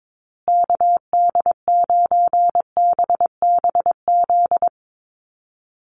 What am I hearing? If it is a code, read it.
Morse code, heard as a keyed tone, about 22 wpm, 700 Hz. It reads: KD9BBZ
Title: USA Call Signs Repeated 3x 22wpm